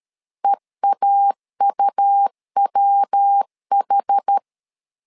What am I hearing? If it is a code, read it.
Morse code, heard as a keyed tone, about 13 wpm, 800 Hz. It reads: EAUWH
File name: LEZIONI / 4 Q 12wpm 1833